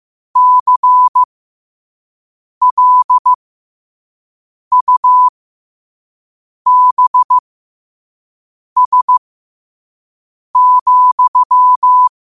Morse code, heard as a keyed tone, about 15 wpm, spaced as for 5 wpm, 1000 Hz. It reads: CLUBS,
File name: qotdfiles / 5wpm